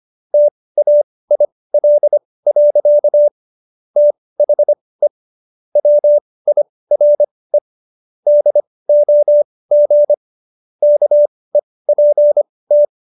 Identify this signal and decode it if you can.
Morse code, heard as a keyed tone, about 25 wpm, spaced as for 18 wpm, 590 Hz. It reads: TAIL. THE WIRE DOG KEPT